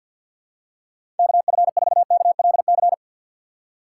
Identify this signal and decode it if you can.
Morse code, heard as a keyed tone, about 52 wpm, 700 Hz. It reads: KF4KLC
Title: USA Call Signs - 6 Characters - 3x Character Spacing - 50wpm